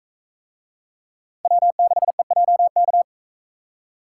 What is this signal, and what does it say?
Morse code, heard as a keyed tone, about 41 wpm, 700 Hz. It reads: W6EJK